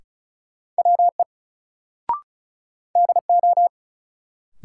Morse code, heard as a keyed tone, about 36 wpm, 700 Hz. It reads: WE DO
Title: Sentences from Top 100 Words - New Word Drill - 35wpm